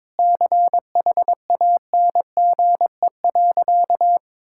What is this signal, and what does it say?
Morse code, heard as a keyed tone, about 22 wpm, 700 Hz. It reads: CHANGE.
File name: practice / 22wpm / Sentences with 8-Word Phrases - 22wpm